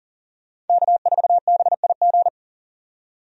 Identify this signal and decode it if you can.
Morse code, heard as a keyed tone, about 41 wpm, 700 Hz. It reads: K4BIG
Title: USA Call Signs - 5 Characters - 40wpm